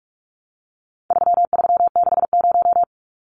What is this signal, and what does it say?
Morse code, heard as a keyed tone, about 44 wpm, 700 Hz. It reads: 3360
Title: Numbers - 4-Digit 45wpm